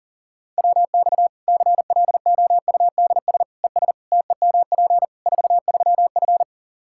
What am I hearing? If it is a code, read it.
Morse code, heard as a keyed tone, about 40 wpm, 700 Hz. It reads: WX CLOUDS ES TEMP 43F
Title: K1USN SST QSO Practice 40wpm